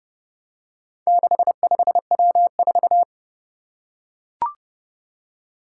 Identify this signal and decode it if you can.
Morse code, heard as a keyed tone, about 30 wpm, 700 Hz. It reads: 65W4
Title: Four Challenging Letter-Number Combinations 30wpm